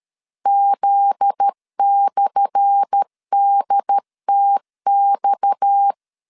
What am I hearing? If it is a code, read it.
Morse code, heard as a keyed tone, about 13 wpm, 800 Hz. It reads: Z/DTX
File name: LEZIONI / 10 SIGNS / 10A SLASH 12wpm 2314